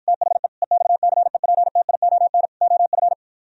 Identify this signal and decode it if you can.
Morse code, heard as a keyed tone, about 53 wpm, 700 Hz. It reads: THE EXCEPTION OF